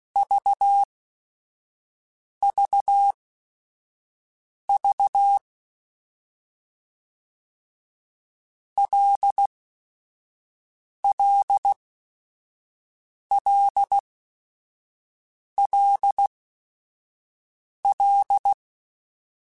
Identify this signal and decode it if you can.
Morse code, heard as a keyed tone, about 16 wpm, spaced as for 5 wpm, 800 Hz. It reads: VVV LLLLL
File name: Basic Sets / Basic Letter Groups Collective7 / LFCYQ@5